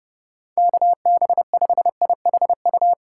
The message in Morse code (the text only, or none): KB5IHU